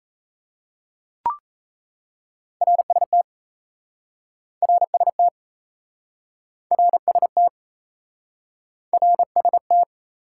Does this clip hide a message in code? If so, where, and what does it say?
Yes: Morse code from 2.6 to 9.8 s, heard as a keyed tone, about 37 wpm, 700 Hz. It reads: RST RST RST RST